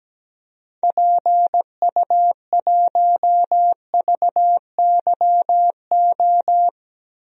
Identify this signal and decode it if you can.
Morse code, heard as a keyed tone, about 17 wpm, 700 Hz. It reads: PU1VYO
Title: DXCC-100 Call Signs - 6 Characters - Repeated 2x - 17wpm